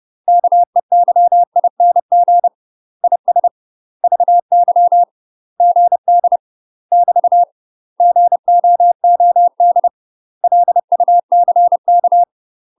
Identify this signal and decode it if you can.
Morse code, heard as a keyed tone, about 30 wpm, 700 Hz. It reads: KEYING IS VY GD <BT> GOOD LUCK
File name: UK MASTER QSOs TRAINING / 30WPM / UK MASTER QSOs 0013